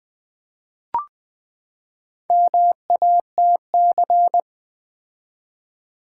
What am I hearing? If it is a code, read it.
Morse code, heard as a keyed tone, about 20 wpm, 700 Hz. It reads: MATC
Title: Build A Word - 4 Letters At A Time - 8-Letter Words - 20wpm